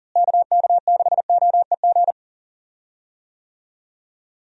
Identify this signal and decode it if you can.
Morse code, heard as a keyed tone, about 40 wpm, 700 Hz. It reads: KK6OEG
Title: USA Call Signs - 6 Characters - Repeated 3x - 40wpm